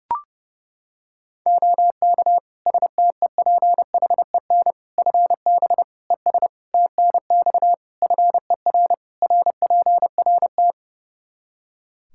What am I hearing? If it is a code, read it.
Morse code, heard as a keyed tone, about 30 wpm, 700 Hz. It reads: OK STEPHEN FB ES TNX FER RPRT
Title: Practice QSO - Exchange 2 - Info - 30wpm